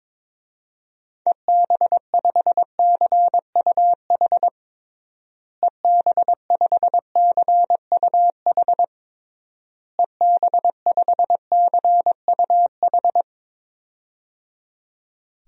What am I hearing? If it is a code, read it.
Morse code, heard as a keyed tone, about 21 wpm, 700 Hz. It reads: EB5CUH EB5CUH EB5CUH
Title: DXCC-100 Call Signs - 6 Characters - Repeated 3x - 22wpm